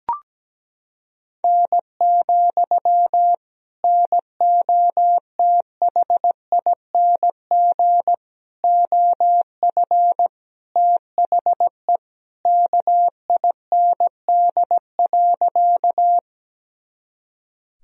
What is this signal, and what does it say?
Morse code, heard as a keyed tone, about 17 wpm, 700 Hz. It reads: N, NOTHING OF THE KIND.
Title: Sentences from Top 500 Words and Abbreviations 17wpm